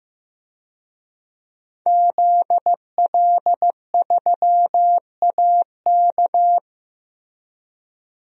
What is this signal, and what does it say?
Morse code, heard as a keyed tone, about 15 wpm, 700 Hz. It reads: ZL3AK
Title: DXCC-100 Call Signs - 5 Characters - Repeated 2x - 15wpm